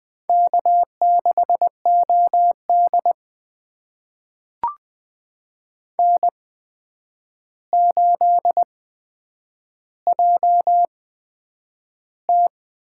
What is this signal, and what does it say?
Morse code, heard as a keyed tone, about 20 wpm, spaced as for 9 wpm, 700 Hz. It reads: K6OD N8JT
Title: USA Call Signs - 4 Characters - 8x Character Spacing - 20wpm